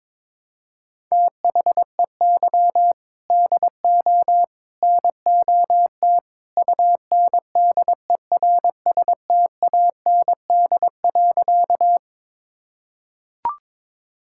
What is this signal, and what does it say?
Morse code, heard as a keyed tone, about 22 wpm, 700 Hz. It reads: THEY DO NOT UNDERSTAND.
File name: practice / 22wpm / Sentences with 3-Word Phrases - 22wpm